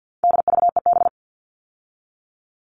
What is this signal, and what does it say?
Morse code, heard as a keyed tone, about 50 wpm, 700 Hz. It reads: DVEB
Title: All Cut Numbers - 4-Digit 50wpm